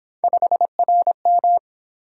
Morse code, heard as a keyed tone, about 27 wpm, 700 Hz. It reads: 5RM
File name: Speed-Racing - Three Letter-Number Combinations - 22wpm